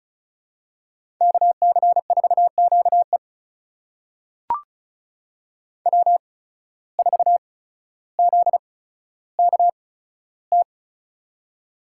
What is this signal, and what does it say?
Morse code, heard as a keyed tone, about 36 wpm, spaced as for 15 wpm, 700 Hz. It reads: KC4QE W4ZKT
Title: USA Call Signs - 8x Character Spacing - 35wpm